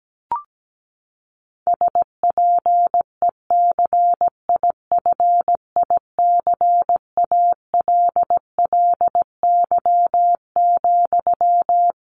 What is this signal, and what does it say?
Morse code, heard as a keyed tone, about 17 wpm, 700 Hz. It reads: SPECIFICALLY,
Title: Sentences from Top 1100 Words 17wpm